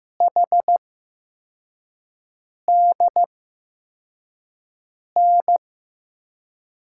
Morse code, heard as a keyed tone, about 15 wpm, spaced as for 4 wpm, 700 Hz. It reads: HDN
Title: USA Call Signs - 8x Character Spacing - 15wpm